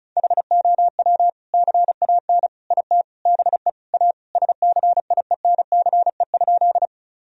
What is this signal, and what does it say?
Morse code, heard as a keyed tone, about 34 wpm, 700 Hz. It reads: HOW CAN IT BE A SCIENCE?
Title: Sentences from Top 1000 Words - Review Part 2 - 35wpm